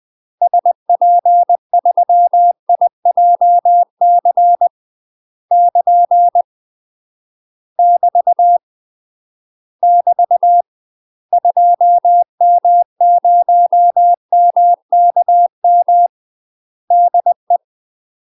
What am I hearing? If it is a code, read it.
Morse code, heard as a keyed tone, about 20 wpm, 700 Hz. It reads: SP3IJC <KN> <BT> <BT> 2M0MKM DE
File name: UK MASTER QSOs TRAINING / 20WPM / UK MASTER QSOs 0014